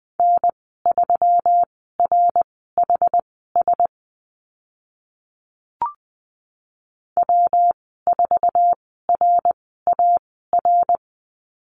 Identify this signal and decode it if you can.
Morse code, heard as a keyed tone, about 20 wpm, spaced as for 15 wpm, 700 Hz. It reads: N3RHS W4RAR
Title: 128 Lesson 21g - Number 6 - Call Signs 20wpm